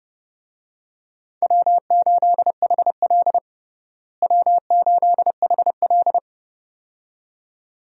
Morse code, heard as a keyed tone, about 30 wpm, 700 Hz. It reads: W8HL W8HL
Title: USA Call Signs - 4 Characters - Repeated 2x - 30wpm